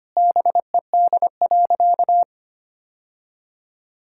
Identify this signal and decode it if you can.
Morse code, heard as a keyed tone, about 25 wpm, 700 Hz. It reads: BED.
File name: practice / 25wpm / Sentences with 3-Word Phrases - 25wpm